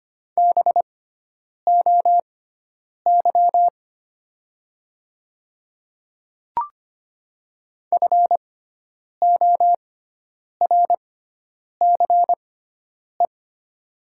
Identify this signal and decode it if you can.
Morse code, heard as a keyed tone, about 25 wpm, spaced as for 9 wpm, 700 Hz. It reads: BOY FORCE
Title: 6x Character Spacing - Top 500 Words - 25wpm